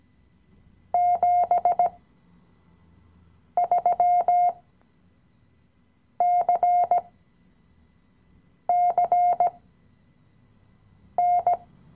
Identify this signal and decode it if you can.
Morse code, heard as a keyed tone, about 17 wpm, spaced as for 5 wpm, 710 Hz. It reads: 73CCN
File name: CW-Online-Kurs / Anfänger / Lektion 017